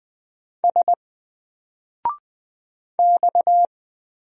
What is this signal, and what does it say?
Morse code, heard as a keyed tone, about 20 wpm, 700 Hz. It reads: S X
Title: Single Character (Letter:Number:Punctuation:Prosign) Alt 20wpm